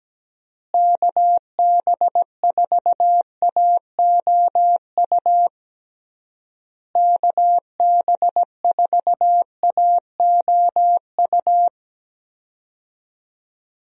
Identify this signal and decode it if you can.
Morse code, heard as a keyed tone, about 17 wpm, 700 Hz. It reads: KB4AOU KB4AOU
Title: USA Call Signs - 6 Characters - Repeated 3x - 17wpm